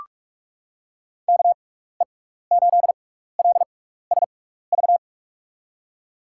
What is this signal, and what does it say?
Morse code, heard as a keyed tone, about 42 wpm, spaced as for 15 wpm, 700 Hz. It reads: KE8LSV
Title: USA Call Signs - 6x Character Spacing - 45wpm